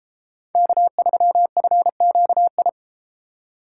Morse code, heard as a keyed tone, about 31 wpm, 700 Hz. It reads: K3FQI